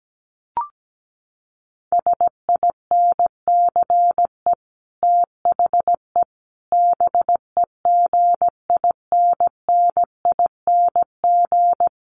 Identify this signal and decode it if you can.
Morse code, heard as a keyed tone, about 17 wpm, 700 Hz. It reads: SINCE THE BEGINNING
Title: Sets of 5 Words Encore2 17wpm